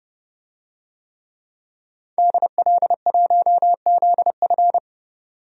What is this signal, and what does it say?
Morse code, heard as a keyed tone, about 30 wpm, 700 Hz. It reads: DL1ZF
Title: DXCC-100 Call Signs - 5 Characters - 30wpm